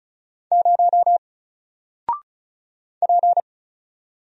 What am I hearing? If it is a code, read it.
Morse code, heard as a keyed tone, about 36 wpm, 700 Hz. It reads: 0 P
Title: Single Character (Letter:Number:Punctuation:Prosign) 35wpm